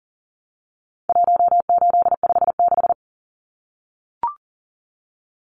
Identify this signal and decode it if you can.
Morse code, heard as a keyed tone, about 40 wpm, 700 Hz. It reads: 1856
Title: Numbers - 4-Digit 40wpm